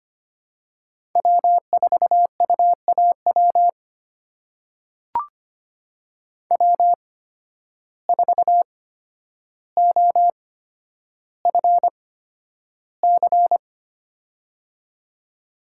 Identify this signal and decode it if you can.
Morse code, heard as a keyed tone, about 25 wpm, spaced as for 11 wpm, 700 Hz. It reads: W4UAW W4OFC
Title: USA Call Signs - 8x Character Spacing - 25wpm